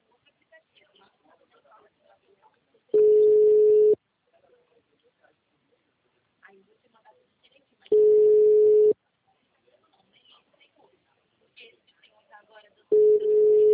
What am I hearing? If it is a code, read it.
Ringback tone (Europe, 425 Hz), with speech.
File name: out- -2020- - - .51189